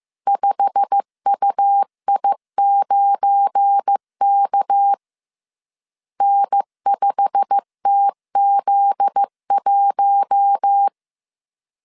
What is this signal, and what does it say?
Morse code, heard as a keyed tone, about 15 wpm, 800 Hz. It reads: 5UI9K N5TZ1
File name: CW / DET / 1 9 1500C S A 14wpm 3222